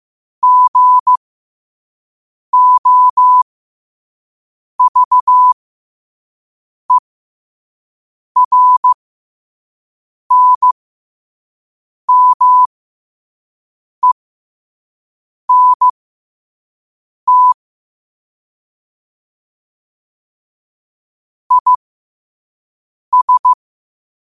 Morse code, heard as a keyed tone, about 15 wpm, spaced as for 5 wpm, 1000 Hz. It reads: GOVERNMENT IS
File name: qotdfiles / 5wpm